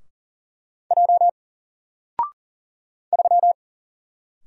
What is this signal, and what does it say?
Morse code, heard as a keyed tone, about 40 wpm, 700 Hz. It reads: J 3
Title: Single Character (Letter:Number:Punctuation:Prosign) Alt 40wpm